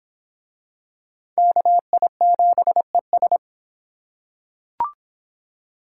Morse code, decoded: KI7ES

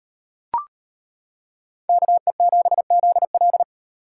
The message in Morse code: KE7ZL

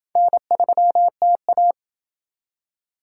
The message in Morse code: N3TA

